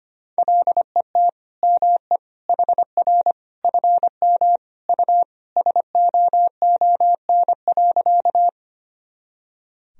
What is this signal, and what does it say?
Morse code, heard as a keyed tone, about 25 wpm, 700 Hz. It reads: LET ME HR FM U SOON.
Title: Sentences from Top 400 Words and Abbreviations 25wpm